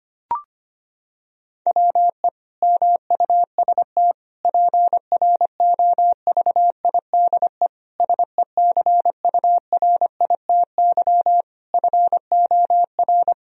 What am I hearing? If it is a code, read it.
Morse code, heard as a keyed tone, about 25 wpm, 700 Hz. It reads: WE MUST PROVIDE SECURITY FOR
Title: Sentences from Top 1000 Words - Review Part 4 - 25wpm